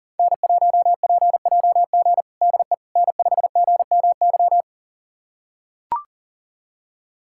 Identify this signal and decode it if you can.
Morse code, heard as a keyed tone, about 40 wpm, 700 Hz. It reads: N1PJG DE N5GMY
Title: USA Callsign DE Callsign - 5 Characters - 40wpm